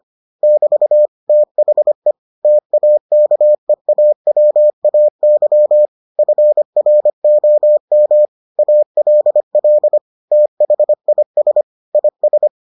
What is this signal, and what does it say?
Morse code, heard as a keyed tone, about 25 wpm, 600 Hz. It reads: <BT> THE TAKEAWAY FROM ALL THIS IS